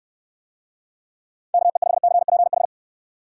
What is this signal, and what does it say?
Morse code, heard as a keyed tone, about 60 wpm, 700 Hz. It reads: KE5CPH